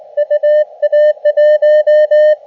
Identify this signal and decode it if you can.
Morse code, heard as a keyed tone, about 19 wpm, 600 Hz. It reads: UA1
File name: Expedition / SWL UT5NM 20.12.2012 VANGUARD UA1CEG p IT9IFI